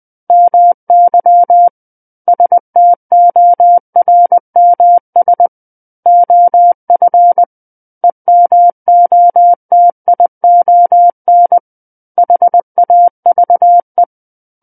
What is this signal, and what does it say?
Morse code, heard as a keyed tone, about 20 wpm, 700 Hz. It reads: MY STORMS OF EMOTION HAVE
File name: 20 WPM